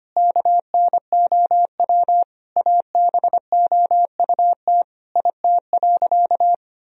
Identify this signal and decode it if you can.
Morse code, heard as a keyed tone, about 25 wpm, 700 Hz. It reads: KNOW ABOUT IT.
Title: Sentences from Top 1000 Words - Review Part 1 - 25wpm